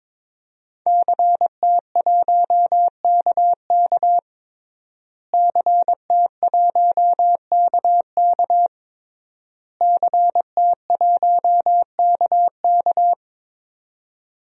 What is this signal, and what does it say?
Morse code, heard as a keyed tone, about 22 wpm, 700 Hz. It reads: CT1KK CT1KK CT1KK
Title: DXCC-100 Call Signs - 5 Characters - Repeated 3x - 22wpm